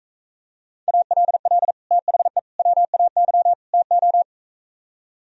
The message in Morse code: ALL THE WAY TO

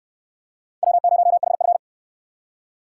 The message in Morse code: W0HF

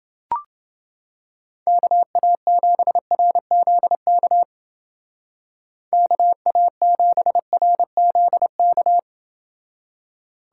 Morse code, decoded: KA7RZK KA7RZK

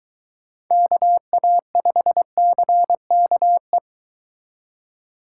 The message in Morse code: KA5CKE

